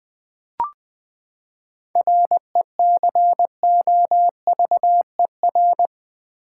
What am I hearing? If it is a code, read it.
Morse code, heard as a keyed tone, about 21 wpm, 700 Hz. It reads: RECOVER
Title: Seven Letter Words - 20wpm